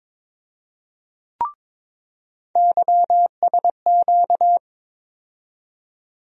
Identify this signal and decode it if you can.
Morse code, heard as a keyed tone, about 22 wpm, 700 Hz. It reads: YSQ
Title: USA Call Sign Postfixes - 3 Letters 22wpm